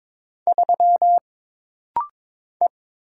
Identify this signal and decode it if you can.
Morse code, heard as a keyed tone, about 22 wpm, 700 Hz. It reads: 3 E
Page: This rapid-fire timing is geared to push your ICR (Instant Character Recognition) skills.